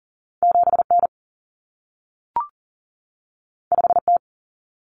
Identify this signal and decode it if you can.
Morse code, heard as a keyed tone, about 40 wpm, 700 Hz. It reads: 7N 5T
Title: Common Cut Numbers - 2-Digit 40wpm